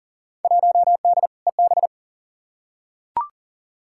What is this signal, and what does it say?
Morse code, heard as a keyed tone, about 37 wpm, 700 Hz. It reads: 1D EB
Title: Field Day Exchanges - 40wpm